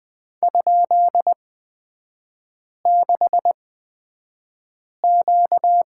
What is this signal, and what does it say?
Morse code, heard as a keyed tone, about 20 wpm, spaced as for 5 wpm, 700 Hz. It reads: ?6Q
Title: 236 Lesson 38e - Letter q - Three Characters 20wpm